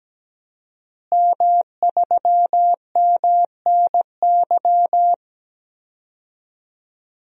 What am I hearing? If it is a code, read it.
Morse code, heard as a keyed tone, about 17 wpm, 700 Hz. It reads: M3MNY
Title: DXCC-100 Call Signs - 5 Characters - Repeated 3x - 17wpm